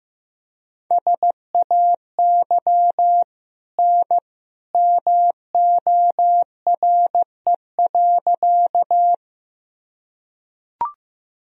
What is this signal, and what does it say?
Morse code, heard as a keyed tone, about 15 wpm, 700 Hz. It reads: SAY N MORE.
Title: Sentences from Top 200 Words and Abbreviations 15wpm